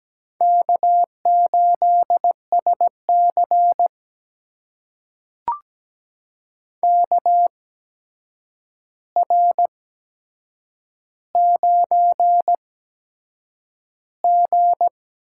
Morse code, heard as a keyed tone, about 17 wpm, spaced as for 7 wpm, 700 Hz. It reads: K8SC KR9G
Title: USA Call Signs - 4 Characters - 8x Character Spacing - 17wpm